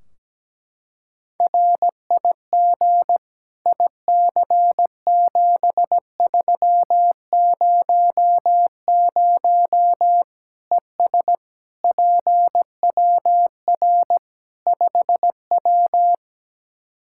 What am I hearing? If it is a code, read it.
Morse code, heard as a keyed tone, about 17 wpm, 700 Hz. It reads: RIG IC7300 ES PWR 5W
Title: Practice QSO - Exchange 2 - Info - 17wpm